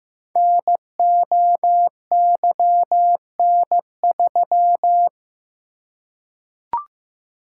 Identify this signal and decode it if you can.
Morse code, heard as a keyed tone, about 15 wpm, 700 Hz. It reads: NOYN3
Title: Code Groups - 5 Letter:Numbers - 15wpm